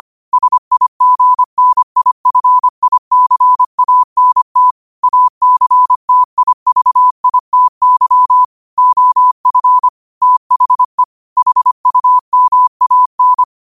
Morse code, heard as a keyed tone, about 25 wpm, 1000 Hz. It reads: SIGNIFICANT ACTIVITY OF THE HUMAN
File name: qotdfiles / 25wpm